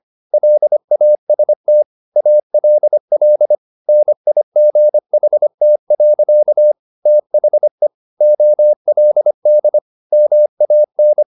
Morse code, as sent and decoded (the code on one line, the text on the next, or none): .-.. .- ... - / .- .-.. .-.. / -. .. --. .... - .-.-.- / - .... / --- .-.. -.. / -- .- -.
LAST ALL NIGHT. THE OLD MAN